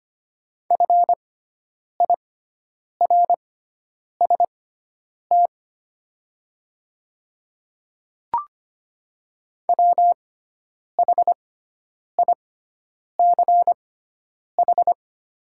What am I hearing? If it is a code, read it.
Morse code, heard as a keyed tone, about 25 wpm, spaced as for 9 wpm, 700 Hz. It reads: FIRST WHICH
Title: 6x Character Spacing - Top 100 Words - 25wpm